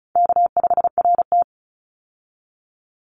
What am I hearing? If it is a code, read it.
Morse code, heard as a keyed tone, about 36 wpm, 700 Hz. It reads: K5RT
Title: Call Signs - 35wpm